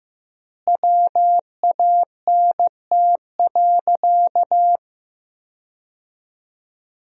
Morse code, heard as a keyed tone, about 15 wpm, 700 Hz. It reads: WANT.
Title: 4x Word-Spacing - Sentences from Top 200 Words - 15wpm